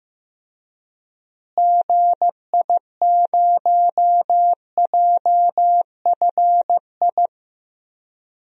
Morse code, heard as a keyed tone, about 15 wpm, 700 Hz. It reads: GI0JFI
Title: DXCC-100 Call Signs - 6 Characters - Repeated 3x - 15wpm